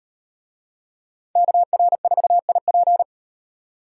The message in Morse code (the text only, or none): KR4IP